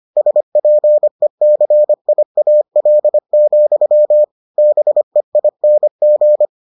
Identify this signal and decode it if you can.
Morse code, heard as a keyed tone, about 25 wpm, 600 Hz. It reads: SPECIAL, BEING